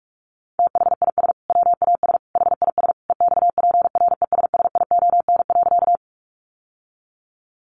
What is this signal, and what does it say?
Morse code, heard as a keyed tone, about 44 wpm, 700 Hz. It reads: THIS WAS HIS EXPRESSION.
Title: Sentences from Top 1200 Words 45wpm